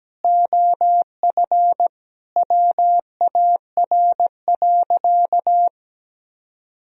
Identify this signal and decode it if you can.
Morse code, heard as a keyed tone, about 17 wpm, 700 Hz. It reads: OF WAR.
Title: Sentences from Top 1000 Words - Review Part 3 - 17wpm